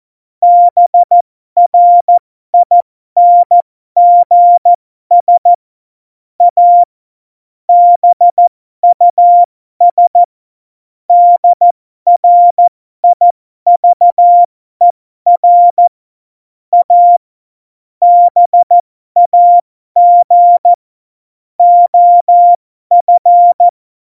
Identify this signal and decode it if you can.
Morse code, heard as a keyed tone, about 14 wpm, spaced as for 12 wpm, 700 Hz. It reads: BRINGS A BUS DRIVER A BAG OF
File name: Morsekurs / Witze